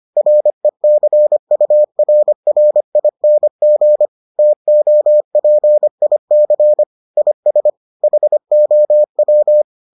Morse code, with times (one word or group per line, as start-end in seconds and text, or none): 0.2-4.1 RECURRING
4.4-6.8 TOPIC
7.2-7.7 IS
8.0-9.6 HOW